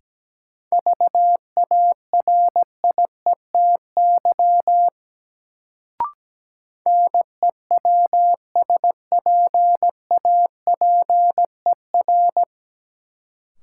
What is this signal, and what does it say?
Morse code, heard as a keyed tone, about 17 wpm, 700 Hz. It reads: VARIETY NEWSPAPER
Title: Sentences from Top 1000 Words - Review Part 4 - New Word Drill - 17wpm